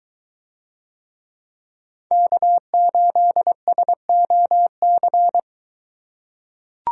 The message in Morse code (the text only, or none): K8SOC